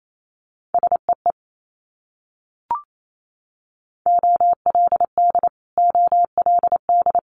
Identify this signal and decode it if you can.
Morse code, heard as a keyed tone, about 28 wpm, 700 Hz. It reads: SEE OLD OLD
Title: Multi-letter Phonemes Repeated 3x 28wpm